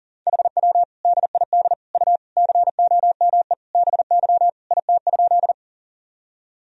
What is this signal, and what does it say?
Morse code, heard as a keyed tone, about 40 wpm, 700 Hz. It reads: HW DID U COME BY IT?
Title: Sentences from Top 100 Words and Abbreviations 40wpm